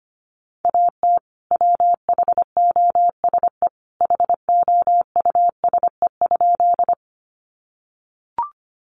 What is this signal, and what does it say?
Morse code, heard as a keyed tone, about 25 wpm, 700 Hz. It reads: AT WHOSE HOUSE?
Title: Sentences from Top 1200 Words 25wpm